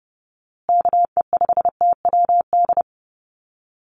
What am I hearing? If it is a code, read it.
Morse code, heard as a keyed tone, about 30 wpm, 700 Hz. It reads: KE5TWD
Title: USA Call Signs Repeated 3x 30wpm